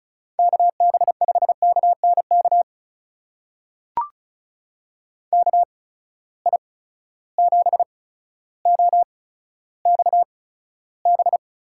Morse code, heard as a keyed tone, about 36 wpm, spaced as for 15 wpm, 700 Hz. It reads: KB5KNK KI7OXB
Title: USA Call Signs - 6 Characters - 8x Character Spacing - 35wpm